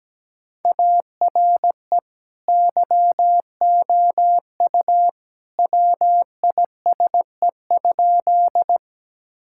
Morse code, heard as a keyed tone, about 17 wpm, 700 Hz. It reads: ARE YOU WISE?
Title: Sentences from Top 1600 Words 17wpm